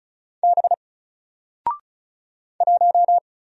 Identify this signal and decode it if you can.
Morse code, heard as a keyed tone, about 36 wpm, 700 Hz. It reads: B 1
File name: Single-Letter-Number-35wpm